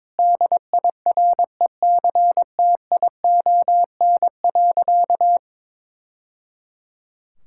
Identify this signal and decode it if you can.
Morse code, heard as a keyed tone, about 22 wpm, 700 Hz. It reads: DIRECTION.
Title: Sentences from Top 1000 Words - Review Part 5 - 22wpm